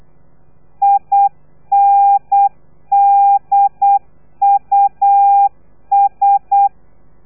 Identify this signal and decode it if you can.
Morse code, heard as a keyed tone, about 8 wpm, 800 Hz. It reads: INDUS
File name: telegrafia / i2tfj / Lez3 35